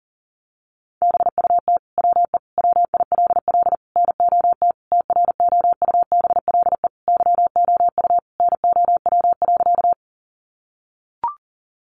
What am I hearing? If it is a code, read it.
Morse code, heard as a keyed tone, about 40 wpm, 700 Hz. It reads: BUT WE WILL NOT TROUBLE YOU NOW.
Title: Sentences from Top 1200 Words 40wpm